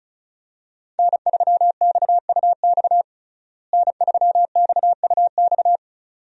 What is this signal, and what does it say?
Morse code, heard as a keyed tone, about 36 wpm, 700 Hz. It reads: N3XUX N3XUX
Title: USA Call Signs - 5 Characters - Repeated 2x - 35wpm